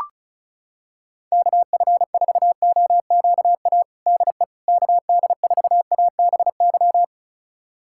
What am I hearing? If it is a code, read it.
Morse code, heard as a keyed tone, about 34 wpm, 700 Hz. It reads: KF4OQA DE KD4ABY
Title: USA Callsign DE Callsign 35wpm